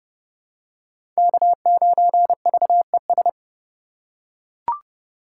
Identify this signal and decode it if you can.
Morse code, heard as a keyed tone, about 30 wpm, 700 Hz. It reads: K9VES
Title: USA Call Signs - 5 Characters - 2x Character Spacing - 30wpm